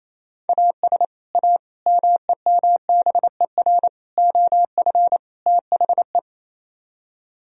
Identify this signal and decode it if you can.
Morse code, heard as a keyed tone, about 34 wpm, 700 Hz. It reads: AS A MEMBER OF THE
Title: Speed-Racing - Sets of 5 Words 28wpm